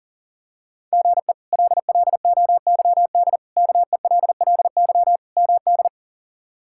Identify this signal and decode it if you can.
Morse code, heard as a keyed tone, about 40 wpm, 700 Hz. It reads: GE LLOYD KELLY MD
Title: K1USN SST QSO Practice - 40wpm